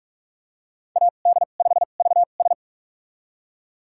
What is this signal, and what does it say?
Morse code, heard as a keyed tone, about 48 wpm, spaced as for 33 wpm, 700 Hz. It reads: AD5VS
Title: USA Call Signs - 5 Characters - 2x Character Spacing - 45wpm